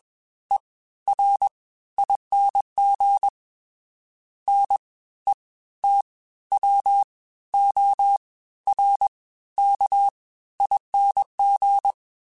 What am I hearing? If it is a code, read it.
Morse code, heard as a keyed tone, about 20 wpm, spaced as for 14 wpm, 800 Hz. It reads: ERING NETWORKING